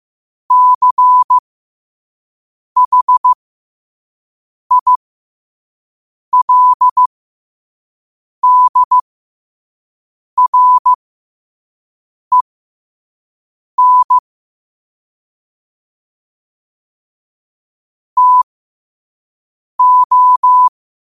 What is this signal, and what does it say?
Morse code, heard as a keyed tone, about 15 wpm, spaced as for 5 wpm, 1000 Hz. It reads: CHILDREN TO